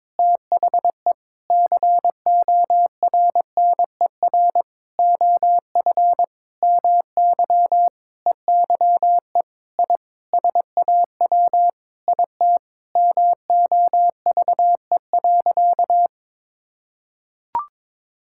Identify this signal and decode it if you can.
Morse code, heard as a keyed tone, about 22 wpm, 700 Hz. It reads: THE CORNER OF MY EYE I SAW IT MOVE.
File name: practice / 22wpm / Sentences with 8-Word Phrases - 22wpm